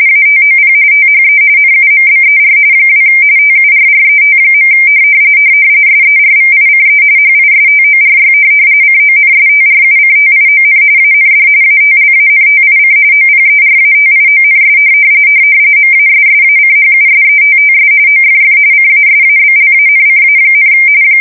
RTTY, about 45.45 baud. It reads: ATOP THE NUMBER 2 TURRET. SERVICE HISTORY CONSTRUCTION AND ATLANTIC OPERATIONS MASSACHUSETTS OFF CASABLANCA,